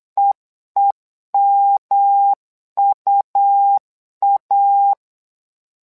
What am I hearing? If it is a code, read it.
Morse code, heard as a keyed tone, about 8 wpm, 800 Hz. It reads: EEMUA
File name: LEZIONI / 3 MWRO / 3 MWRO 8wpm 2606